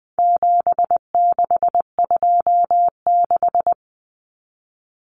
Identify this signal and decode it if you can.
Morse code, heard as a keyed tone, about 20 wpm, 700 Hz. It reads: 7626
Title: Numbers - 4-Digit 20wpm